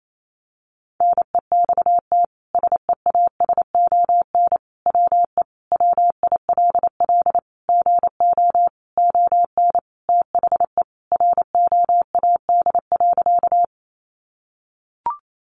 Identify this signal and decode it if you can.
Morse code, heard as a keyed tone, about 28 wpm, 700 Hz. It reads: NEXT SEASON WE WILL GO ON THE ROAD.
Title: Sentences from Top 1400 Words 28wpm